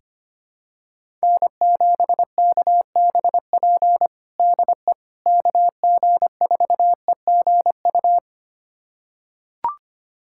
Morse code, heard as a keyed tone, about 25 wpm, 700 Hz. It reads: N7KBP DE KG4EGU
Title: USA Callsign DE Callsign 25wpm